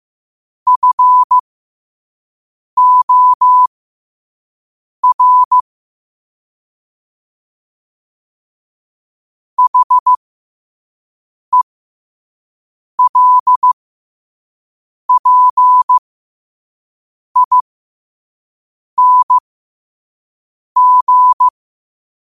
Morse code, heard as a keyed tone, about 15 wpm, spaced as for 5 wpm, 1000 Hz. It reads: FOR HELPING
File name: qotdfiles / 5wpm